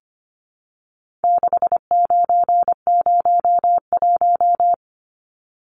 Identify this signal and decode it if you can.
Morse code, heard as a keyed tone, about 25 wpm, 700 Hz. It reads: 6901
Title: Numbers - 4-Digit 25wpm